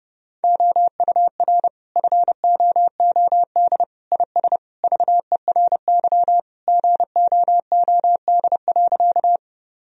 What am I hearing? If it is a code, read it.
Morse code, heard as a keyed tone, about 30 wpm, 700 Hz. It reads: OUR FOOD IS VERY GOOD.
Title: Sentences from Top 1000 Words - Review Part 1 - 30wpm